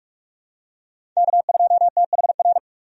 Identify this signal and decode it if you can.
Morse code, heard as a keyed tone, about 45 wpm, 700 Hz. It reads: K2THR
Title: USA Call Signs - 5 Characters - 45wpm